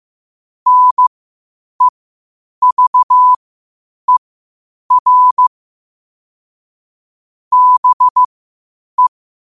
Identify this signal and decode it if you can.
Morse code, heard as a keyed tone, about 15 wpm, spaced as for 8 wpm, 1000 Hz. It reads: NEVER BE
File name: qotdfiles / 8wpm